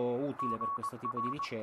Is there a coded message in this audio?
Yes: Morse code from 0.4 to 1.5 s, heard as a keyed tone, about 21 wpm, 1120 Hz, with speech. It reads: GR